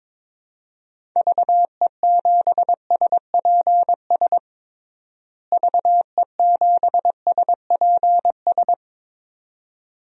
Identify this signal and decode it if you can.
Morse code, heard as a keyed tone, about 22 wpm, 700 Hz. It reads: VE7SPS VE7SPS
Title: DXCC-100 Call Signs - 6 Characters - Repeated 2x - 22wpm